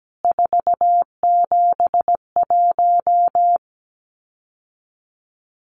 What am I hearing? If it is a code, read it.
Morse code, heard as a keyed tone, about 17 wpm, 700 Hz. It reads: 471
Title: CWT Exchange 17wpm